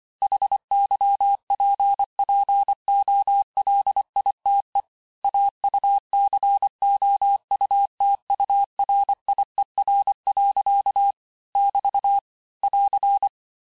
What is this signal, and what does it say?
Morse code, heard as a keyed tone, about 25 wpm, 800 Hz. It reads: HYPPOLITE AUCOUTURIER. <BT> <AR>
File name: cours-du-lundi files / cours-1589-texte-clair-vitesse-18mpm-puis-23mpm